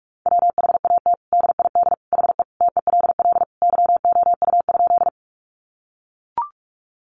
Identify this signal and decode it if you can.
Morse code, heard as a keyed tone, about 44 wpm, 700 Hz. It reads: WHAT DID HE TELL YOU?
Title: Sentences from Top 200 Words 45wpm